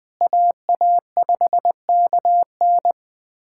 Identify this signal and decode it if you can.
Morse code, heard as a keyed tone, about 20 wpm, 700 Hz. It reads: AA5KN
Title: USA Callsign DE Callsign - 5 Characters - 20wpm